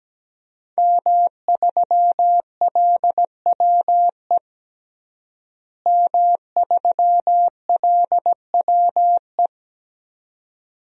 Morse code, heard as a keyed tone, about 17 wpm, 700 Hz. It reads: M3LWE M3LWE
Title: DXCC-100 Call Signs - 5 Characters - Repeated 2x - 17wpm